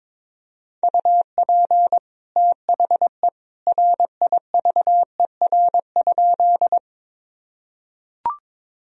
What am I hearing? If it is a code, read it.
Morse code, heard as a keyed tone, about 22 wpm, 700 Hz. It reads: UP THE RIVER?
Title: Sentences from Top 1000 Words - Review Part 1 - 22wpm